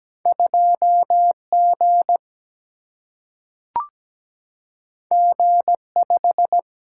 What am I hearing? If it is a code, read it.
Morse code, heard as a keyed tone, about 17 wpm, 700 Hz. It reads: 2 G G 5
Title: Single-Letter Single-Number 17wpm